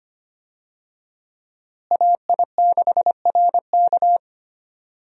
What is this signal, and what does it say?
Morse code, heard as a keyed tone, about 25 wpm, 700 Hz. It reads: AI6RK
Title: USA Call Signs - 5 Characters - Repeated 3x - 25wpm